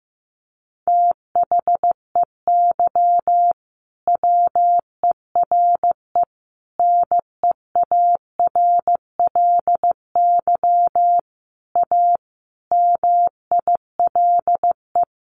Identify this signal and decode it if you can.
Morse code, heard as a keyed tone, about 15 wpm, 700 Hz. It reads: THEY WERE NEARLY A MILE
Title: Sentences from Top 1300 Words 15wpm